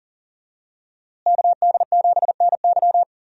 Morse code, heard as a keyed tone, about 40 wpm, 700 Hz. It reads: KD7NY
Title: USA Call Signs - 5 Characters - Repeated 3x - 40wpm